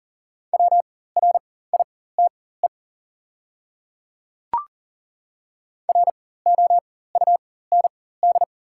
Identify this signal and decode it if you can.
Morse code, heard as a keyed tone, about 40 wpm, spaced as for 19 wpm, 690 Hz. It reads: WRITE ROUND
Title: 4x Character Spacing - Top 300 Words - 40wpm